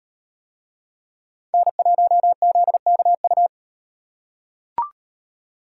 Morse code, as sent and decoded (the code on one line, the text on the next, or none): -. .---- --.. -.- ..-
N1ZKU